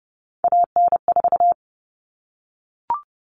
Morse code, heard as a keyed tone, about 30 wpm, 700 Hz. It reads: AN4
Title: All Cut Numbers - 3-Digit 30wpm